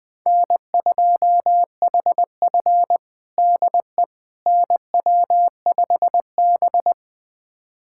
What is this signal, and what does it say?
Morse code, heard as a keyed tone, about 20 wpm, 700 Hz. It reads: N2HF DE NW5B
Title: USA Callsign DE Callsign - 4 Characters - 20wpm